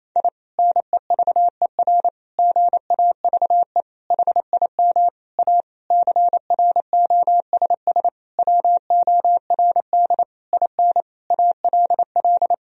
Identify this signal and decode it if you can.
Morse code, heard as a keyed tone, about 28 wpm, 700 Hz. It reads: I NEVER GAVE HIM A CROSS WORD IN ALL
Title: Sentences from Top 1000 Words - Review Part 1 - 28wpm